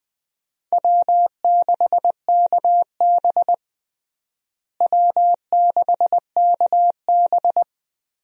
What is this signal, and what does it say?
Morse code, heard as a keyed tone, about 20 wpm, 700 Hz. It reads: W6KB W6KB
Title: USA Call Signs - 4 Characters - Repeated 3x - 20wpm